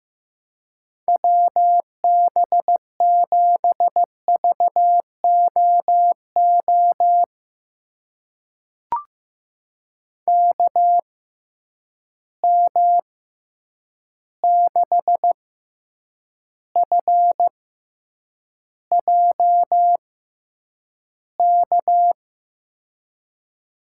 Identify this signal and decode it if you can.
Morse code, heard as a keyed tone, about 15 wpm, spaced as for 8 wpm, 700 Hz. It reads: WB7VOO KM6FJK
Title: USA Call Signs - 6x Character Spacing - 15wpm